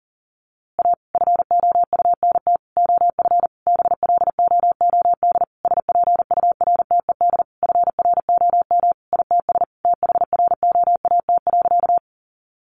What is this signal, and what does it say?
Morse code, heard as a keyed tone, about 40 wpm, 700 Hz. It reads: A FOUNT OF BLOOD SPURTED FROM ITS THROAT.
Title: A Sound Of Thunder 40wpm